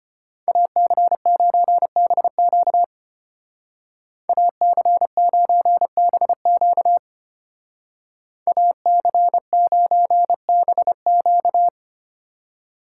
Morse code, heard as a keyed tone, about 32 wpm, 700 Hz. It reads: AC9BQ AC9BQ AC9BQ